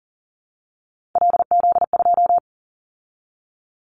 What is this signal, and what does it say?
Morse code, heard as a keyed tone, about 40 wpm, 700 Hz. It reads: LZ2
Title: Callsign Prefixes for Top 25 DXCC - 40wpm